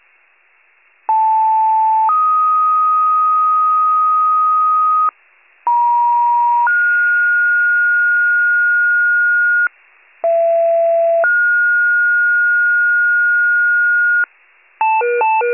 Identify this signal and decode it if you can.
Two-tone page, heard as a one-second tone, then a three-second tone, about 880 then 1250 Hz, about 950 then 1430 Hz, about 670 then 1430 Hz, with noise. It is heard from 1.1 to 14.2 s.
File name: wcalert